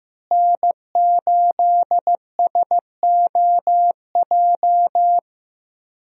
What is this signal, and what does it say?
Morse code, heard as a keyed tone, about 15 wpm, 700 Hz. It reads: N8SOJ
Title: USA Call Signs - 5 Characters - Repeated 2x - 15wpm